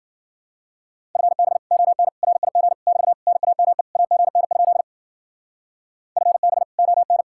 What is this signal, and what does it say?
Morse code, heard as a keyed tone, about 50 wpm, 700 Hz. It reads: FB ON RIG <BT> NAME AGN? FB ON